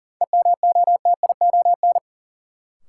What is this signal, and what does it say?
Morse code, heard as a keyed tone, about 40 wpm, 700 Hz. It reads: EMOTION
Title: Sentences from Top 1600 Words - New Word Drill Alt - 40wpm